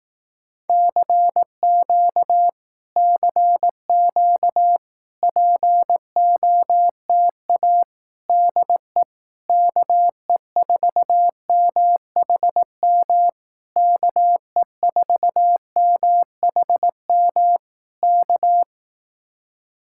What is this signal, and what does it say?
Morse code, heard as a keyed tone, about 18 wpm, 700 Hz. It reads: CQ CQ POTA DE KE4MHM KE4MHM K
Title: Speed-Racing Common QSO Phrases Encore - 15wpm